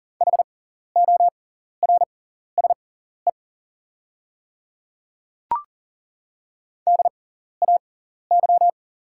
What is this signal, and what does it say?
Morse code, heard as a keyed tone, about 38 wpm, spaced as for 14 wpm, 700 Hz. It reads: HORSE DAY
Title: 6x Character Spacing - Top 500 Words - 40wpm